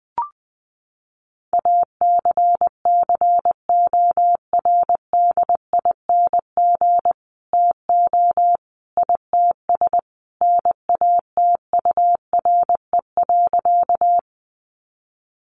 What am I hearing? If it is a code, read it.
Morse code, heard as a keyed tone, about 20 wpm, 700 Hz. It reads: ACCORDING TO ITS NATURE.
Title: Sentences from Top 700 Words 20wpm